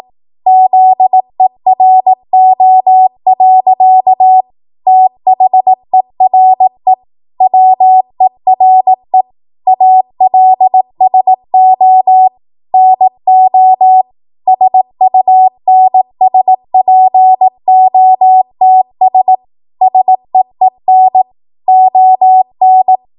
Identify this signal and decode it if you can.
Morse code, heard as a keyed tone, about 18 wpm, 750 Hz. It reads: ZERO. THERE WERE ALSO NO SUNSPOTS SEEN ON
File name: Archive / 18 WPM / 18WPM